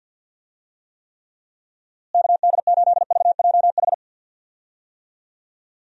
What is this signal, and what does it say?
Morse code, heard as a keyed tone, about 49 wpm, 700 Hz. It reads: KD8VJH
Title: USA Call Signs - 6 Characters - Repeated 2x - 50wpm